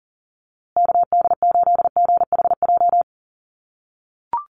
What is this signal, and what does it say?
Morse code, heard as a keyed tone, about 40 wpm, 700 Hz. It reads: KD8GHJ
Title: USA Call Signs Repeated 2x 40wpm